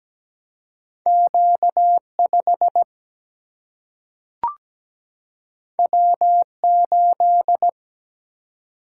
Morse code, heard as a keyed tone, about 17 wpm, 700 Hz. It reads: Q5 W8
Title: Single-Letter Single-Number 17wpm